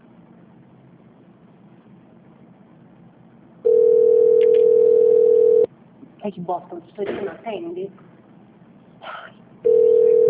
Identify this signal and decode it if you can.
Ringback tone (North America), 440 + 480 Hz, with speech.
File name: out- -2039- - - .4393